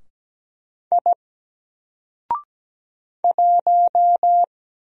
Morse code, heard as a keyed tone, about 17 wpm, 700 Hz. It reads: I 1
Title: Single Letter-Number Alt 17wpm